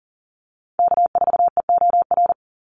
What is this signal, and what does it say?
Morse code, heard as a keyed tone, about 40 wpm, 700 Hz. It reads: K4EOR
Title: Call Signs - 40wpm